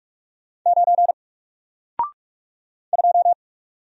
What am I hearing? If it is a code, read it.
Morse code, heard as a keyed tone, about 44 wpm, 700 Hz. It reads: 9 2
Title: Single Letter-Number 45wpm